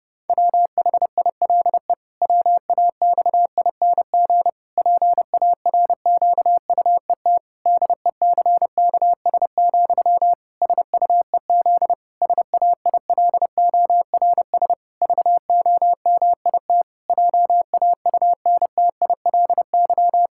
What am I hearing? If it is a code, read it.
Morse code, heard as a keyed tone, about 32 wpm, 700 Hz. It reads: WHILE WAXING PARQUET DECKS, SUEZ SAILORS VOMIT JAUNTILY
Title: Pangram Challenge v2 - 30wpm